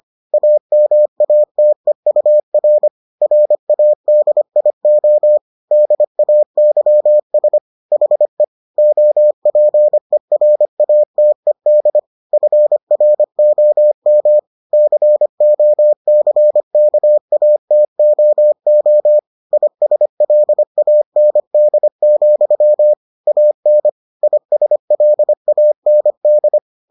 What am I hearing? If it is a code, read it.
Morse code, heard as a keyed tone, about 25 wpm, 600 Hz. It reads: AMATEUR RADIO DAYS HE OPERATED FROM COCKATOO ISLAND, AN ISLAND